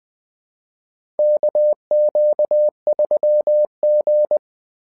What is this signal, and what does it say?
Morse code, heard as a keyed tone, about 20 wpm, 600 Hz. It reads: KQ3G
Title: SOTA QSO Practice 20wpm